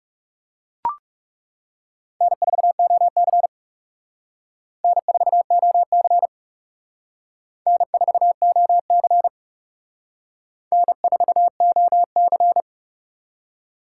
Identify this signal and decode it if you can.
Morse code, heard as a keyed tone, about 37 wpm, 700 Hz. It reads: N4OC N4OC N4OC N4OC
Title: Speed Racing - USA 1x2 Call Signs 30wpm